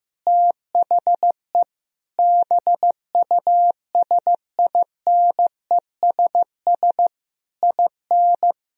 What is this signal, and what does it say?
Morse code, heard as a keyed tone, about 15 wpm, 700 Hz. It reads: THE BUSINESS IN
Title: Sentences from Top 1000 Words - Review Part 2 - 15wpm